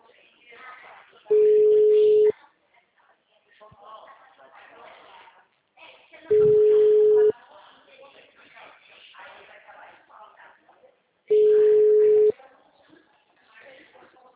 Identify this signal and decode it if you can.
Ringback tone (Europe, 425 Hz), with speech.
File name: out- -2014- - - .3611